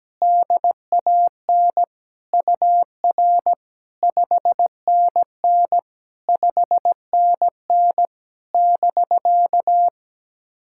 Morse code, heard as a keyed tone, about 17 wpm, 700 Hz. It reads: DAN UR 5NN 5NN <BK>
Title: SOTA QSO Practice 17wpm